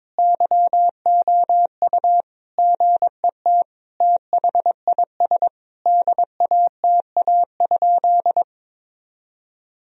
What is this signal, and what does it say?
Morse code, heard as a keyed tone, about 22 wpm, 700 Hz. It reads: YOU GET THIS DATA?
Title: Sentences from Top 1000 Words - Review Part 3 - 22wpm